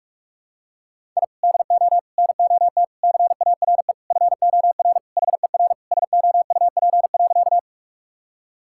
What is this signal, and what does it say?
Morse code, heard as a keyed tone, about 46 wpm, 700 Hz. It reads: I DO NOT CARE FOR HER SOUP.
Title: Sentences from Top 1000 Words - Review Part 6 - 45wpm